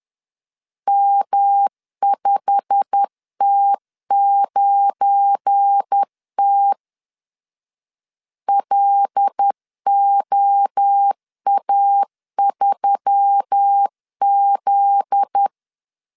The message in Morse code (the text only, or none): M5T9T LOA3Z